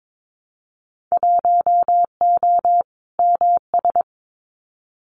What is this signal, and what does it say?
Morse code, heard as a keyed tone, about 22 wpm, 700 Hz. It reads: 1O MS
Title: Winter Field Day Exchanges 22wpm